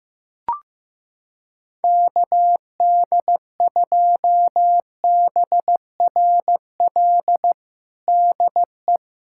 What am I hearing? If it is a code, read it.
Morse code, heard as a keyed tone, about 15 wpm, 700 Hz. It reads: KD2BRL DE
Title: USA Callsign DE Callsign 15wpm